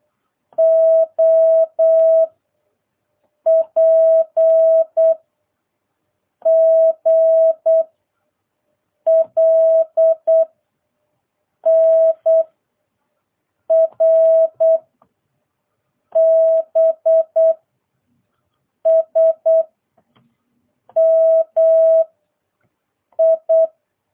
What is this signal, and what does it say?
Morse code, heard as a keyed tone, about 8 wpm, 650 Hz. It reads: O P G L N R B S M I